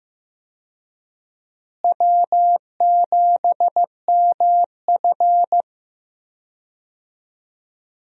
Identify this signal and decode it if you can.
Morse code, heard as a keyed tone, about 15 wpm, 700 Hz. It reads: W7MF
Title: USA Call Signs - 4 Characters - Repeated 2x - 15wpm